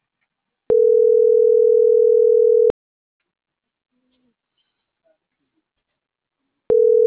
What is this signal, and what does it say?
Ringback tone (North America), 440 + 480 Hz.